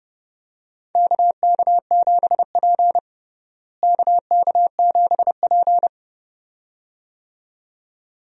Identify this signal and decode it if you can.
Morse code, heard as a keyed tone, about 30 wpm, 700 Hz. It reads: KK7P KK7P
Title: USA Call Signs - 4 Characters - Repeated 2x - 30wpm